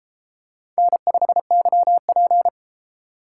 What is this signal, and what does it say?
Morse code, heard as a keyed tone, about 33 wpm, 700 Hz. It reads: N5YP